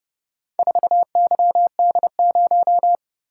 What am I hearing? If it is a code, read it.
Morse code, heard as a keyed tone, about 29 wpm, 700 Hz. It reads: 4YD0